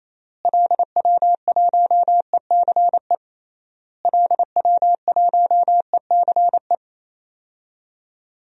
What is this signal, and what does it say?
Morse code, heard as a keyed tone, about 28 wpm, 700 Hz. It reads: LW1ECE LW1ECE
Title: DXCC-100 Call Signs - 6 Characters - Repeated 3x - 28wpm